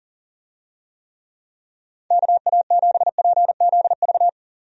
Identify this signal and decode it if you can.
Morse code, heard as a keyed tone, about 38 wpm, 700 Hz. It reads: KA7PZV
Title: USA Call Signs - 6 Characters - 40wpm